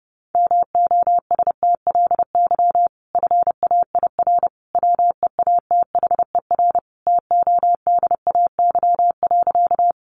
Morse code, heard as a keyed tone, about 30 wpm, 700 Hz. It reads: MOSTLY FAIR WEATHER TODAY.
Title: Sentences from Top 1300 Words 30wpm